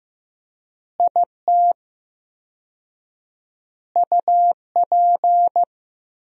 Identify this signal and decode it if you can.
Morse code, heard as a keyed tone, about 15 wpm, 700 Hz. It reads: IT UP
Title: 4x Word-Spacing - Sentences from Top 400 Words - 15wpm